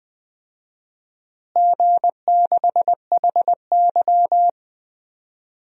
Morse code, heard as a keyed tone, about 20 wpm, 700 Hz. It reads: G6HY
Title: DXCC-100 Call Signs - 4 Characters - Repeated 2x - 20wpm